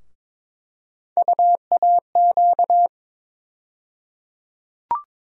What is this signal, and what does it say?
Morse code, heard as a keyed tone, about 22 wpm, 700 Hz. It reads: UAQ
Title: USA Call Sign Postfixes - 3 Letters 22wpm